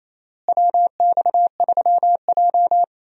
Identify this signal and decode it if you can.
Morse code, heard as a keyed tone, about 28 wpm, 700 Hz. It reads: WX3J
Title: USA Call Signs - 4 Characters - Repeated 2x - 28wpm